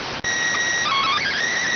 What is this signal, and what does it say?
SSTV (slow-scan television), PD 290.